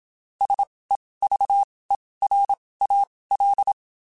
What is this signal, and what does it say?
Morse code, heard as a keyed tone, about 27 wpm, spaced as for 19 wpm, 800 Hz. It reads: SEVERAL
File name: (spaced) / QSOs Spaced-mp3s / kc7haa @25 sp